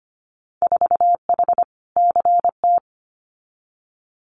Morse code, heard as a keyed tone, about 25 wpm, 700 Hz. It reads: 4H CT
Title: Winter Field Day Exchanges Encore 25wpm